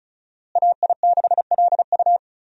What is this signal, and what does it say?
Morse code, heard as a keyed tone, about 36 wpm, 700 Hz. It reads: AI6LU
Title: USA Call Signs - 5 Characters - Repeated 3x - 35wpm